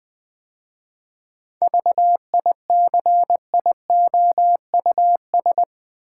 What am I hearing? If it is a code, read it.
Morse code, heard as a keyed tone, about 20 wpm, 700 Hz. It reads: VICIOUS
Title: 7 Letter Words - First 3 Letters Given - 20wpm